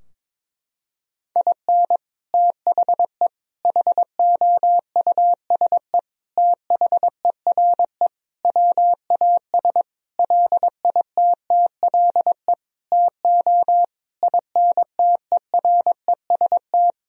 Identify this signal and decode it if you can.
Morse code, heard as a keyed tone, about 22 wpm, 700 Hz. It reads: IN THE HOUSE THERE WAS LITTLE TO INTEREST
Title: Sentences from Top 1000 Words - Review Part 2 - 22wpm